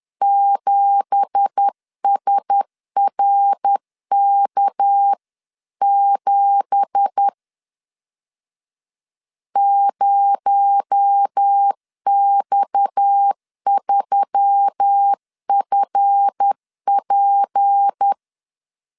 Morse code, heard as a keyed tone, about 10 wpm, 800 Hz. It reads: 7SRK7 0X3FP